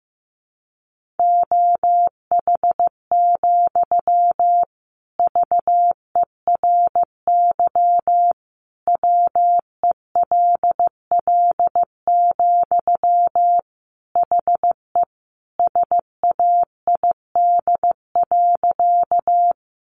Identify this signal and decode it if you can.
Morse code, heard as a keyed tone, about 15 wpm, 700 Hz. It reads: OH, VERY WELL, HE SAID.
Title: Sentences from Top 500 Words 15wpm